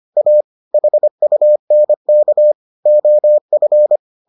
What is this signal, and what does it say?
Morse code, heard as a keyed tone, about 25 wpm, 600 Hz. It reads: A HUNK OF